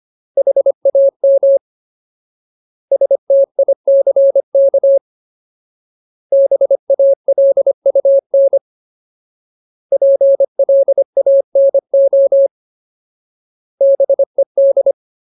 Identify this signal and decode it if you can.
Morse code, heard as a keyed tone, about 25 wpm, 550 Hz. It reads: HAM STICK BALUN PLANO BED
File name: Ultimate Word List / 25WPM Ultimate